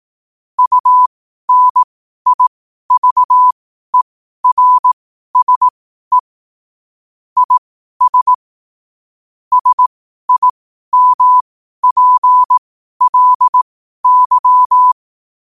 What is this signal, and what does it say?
Morse code, heard as a keyed tone, about 18 wpm, spaced as for 12 wpm, 1000 Hz. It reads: UNIVERSE IS SIMPLY
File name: qotdfiles / 12wpm